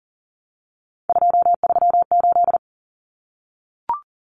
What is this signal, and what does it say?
Morse code, heard as a keyed tone, about 38 wpm, 700 Hz. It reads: 238
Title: Numbers - 3-Digit 40wpm